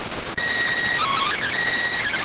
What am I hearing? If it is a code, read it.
SSTV (slow-scan television), PD 290.